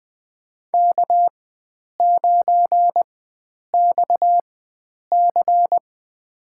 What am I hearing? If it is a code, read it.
Morse code, heard as a keyed tone, about 20 wpm, spaced as for 9 wpm, 700 Hz. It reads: K9XC
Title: USA Call Signs - 4x Character Spacing - 20wpm